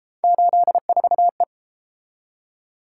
Morse code, heard as a keyed tone, about 30 wpm, 700 Hz. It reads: 84E